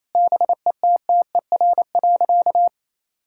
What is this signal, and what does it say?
Morse code, heard as a keyed tone, about 28 wpm, 700 Hz. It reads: BETTER.
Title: Sentences from Top 1000 Words - Review Part 6 - 28wpm